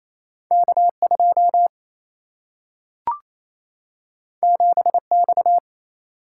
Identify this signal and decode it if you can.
Morse code, heard as a keyed tone, about 28 wpm, 700 Hz. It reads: K 2 7 X
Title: Single-Letter Single-Number 28wpm